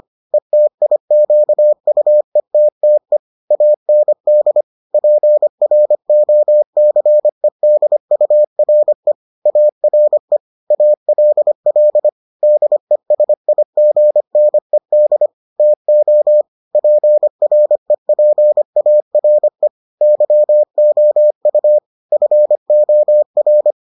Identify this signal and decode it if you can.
Morse code, heard as a keyed tone, about 25 wpm, 600 Hz. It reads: ETIQUETTE AND PROCEDURE ARE ALL DESIGNED TO PREPARE YOU FOR